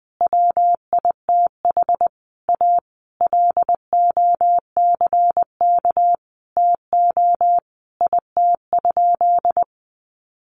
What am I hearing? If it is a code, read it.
Morse code, heard as a keyed tone, about 20 wpm, 700 Hz. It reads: WITH A LOCK TO IT?
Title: Sentences from Top 1300 Words 20wpm